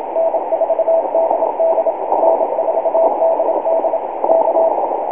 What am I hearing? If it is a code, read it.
Morse code, heard as a keyed tone, about 27 wpm, 660 Hz, with radio noise. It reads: N4DD N4DD K